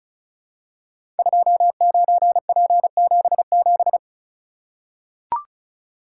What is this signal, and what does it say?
Morse code, heard as a keyed tone, about 36 wpm, 690 Hz. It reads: 29P77
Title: Code Groups - 5 Letter:Numbers - 35wpm